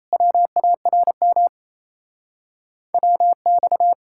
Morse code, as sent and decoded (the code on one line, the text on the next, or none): .-- .- .-. -- / .-- -..-
WARM WX